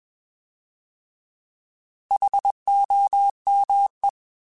Morse code, heard as a keyed tone, about 21 wpm, 800 Hz. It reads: HOME
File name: Alphabetical / Alphabetical mp3s / Hs @20